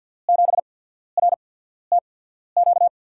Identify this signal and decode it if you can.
Morse code, heard as a keyed tone, about 52 wpm, spaced as for 14 wpm, 700 Hz. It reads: 7RTQ
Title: USA Call Signs - 6 Characters - 8x Character Spacing - 50wpm